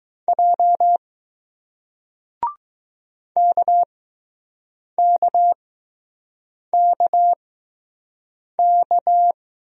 Morse code, heard as a keyed tone, about 20 wpm, spaced as for 6 wpm, 700 Hz. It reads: J KKKK